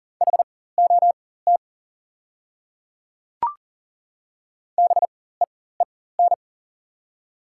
Morse code, heard as a keyed tone, about 40 wpm, spaced as for 19 wpm, 700 Hz. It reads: HOT BEEN
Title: 4x Character Spacing - Top 100 Words - 40wpm